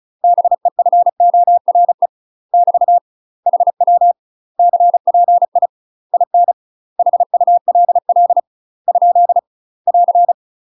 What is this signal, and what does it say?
Morse code, heard as a keyed tone, about 36 wpm, 700 Hz. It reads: BEFORE <BT> HW CPI IN HULL ? <AR>